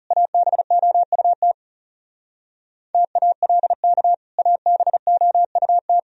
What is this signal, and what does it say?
Morse code, heard as a keyed tone, about 37 wpm, 700 Hz. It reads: ABOUT TALK ABOUT
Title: Speed-Racing - Sets of 2 Words 30wpm